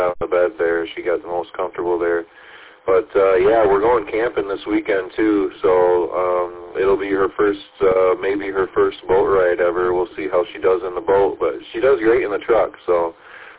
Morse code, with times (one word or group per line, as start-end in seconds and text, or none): none